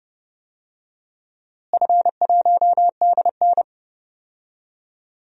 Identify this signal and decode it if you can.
Morse code, heard as a keyed tone, about 30 wpm, 700 Hz. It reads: F1DN
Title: DXCC-100 Call Signs - 4 Characters - Repeated 3x - 30wpm